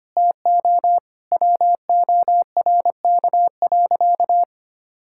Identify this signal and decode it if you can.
Morse code, heard as a keyed tone, about 25 wpm, 700 Hz. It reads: TO WORK.
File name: practice / 25wpm / Sentences with 9-Word Phrases - 25wpm